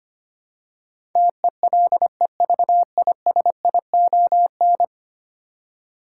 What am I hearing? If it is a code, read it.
Morse code, heard as a keyed tone, about 25 wpm, 700 Hz. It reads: TELEVISION
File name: Ten Letter Words - 25wpm